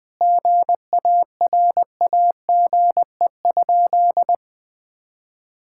Morse code, heard as a keyed tone, about 20 wpm, 700 Hz. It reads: GARAGE?
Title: Sentences from Top 1600 Words 20wpm